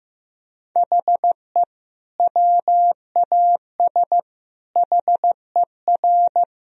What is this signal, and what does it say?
Morse code, heard as a keyed tone, about 14 wpm, 700 Hz. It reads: HE WAS HER
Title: Sentences from Top 1000 Words - Review Part 3 - 15wpm